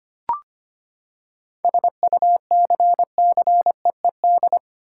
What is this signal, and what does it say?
Morse code, heard as a keyed tone, about 25 wpm, 700 Hz. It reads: SUCCEED
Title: Seven Letter Words - 25wpm